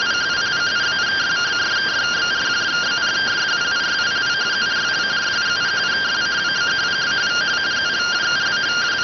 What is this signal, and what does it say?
Two-tone teleprinter signal, about 170 Hz shift.